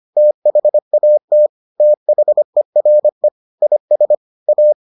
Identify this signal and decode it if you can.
Morse code, heard as a keyed tone, about 25 wpm, 600 Hz. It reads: THAT THERE IS A